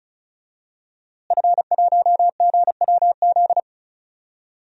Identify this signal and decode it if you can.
Morse code, heard as a keyed tone, about 36 wpm, 700 Hz. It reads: F1GWZ
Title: DXCC-100 Call Signs - 5 Characters - Repeated 2x - 35wpm